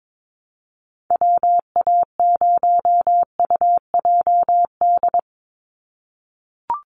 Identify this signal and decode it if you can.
Morse code, heard as a keyed tone, about 22 wpm, 700 Hz. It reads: WA0UJD
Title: USA Call Signs 22wpm